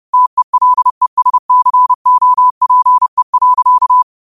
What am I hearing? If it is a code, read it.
Morse code, heard as a keyed tone, about 30 wpm, 1000 Hz. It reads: TELESCOPE.
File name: qotdfiles / 30wpm